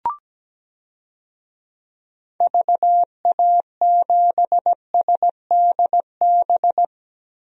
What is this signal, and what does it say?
Morse code, heard as a keyed tone, about 17 wpm, 700 Hz. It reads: VA7SDB
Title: DXCC-100 Call Signs - 6 Characters - Repeated 2x - 17wpm